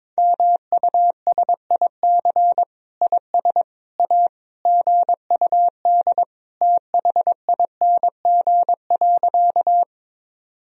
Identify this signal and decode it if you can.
Morse code, heard as a keyed tone, about 21 wpm, 700 Hz. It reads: MUSIC IS A GUD THING.
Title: Sentences from Top 300 Words and Abbreviations 22wpm